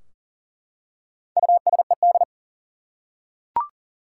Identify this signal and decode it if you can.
Morse code, heard as a keyed tone, about 40 wpm, 700 Hz. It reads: USED
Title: Four Letter Words - 40wpm